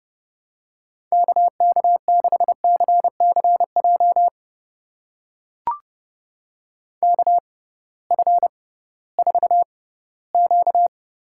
Morse code, heard as a keyed tone, about 32 wpm, spaced as for 18 wpm, 700 Hz. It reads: KK6CCJ KF4Q
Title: USA Call Signs - 6 Characters - 6x Character Spacing - 30wpm